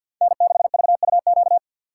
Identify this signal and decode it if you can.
Morse code, heard as a keyed tone, about 48 wpm, 700 Hz. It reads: N6VUQ
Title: USA Call Signs - 5 Characters - 6x Character Spacing - 50wpm